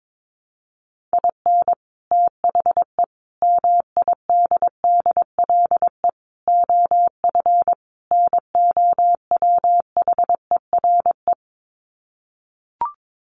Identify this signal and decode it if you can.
Morse code, heard as a keyed tone, about 22 wpm, 700 Hz. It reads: IN THE MIDDLE OF NOWHERE
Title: Sets of 5 Words Encore 22wpm